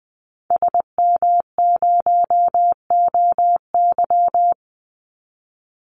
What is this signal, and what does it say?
Morse code, heard as a keyed tone, about 20 wpm, 700 Hz. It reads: SM0OY
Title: Call Signs Repeated 3x - 20wpm